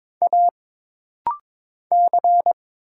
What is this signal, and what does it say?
Morse code, heard as a keyed tone, about 22 wpm, 700 Hz. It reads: A C
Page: This rapid-fire timing is geared to push your ICR (Instant Character Recognition) skills.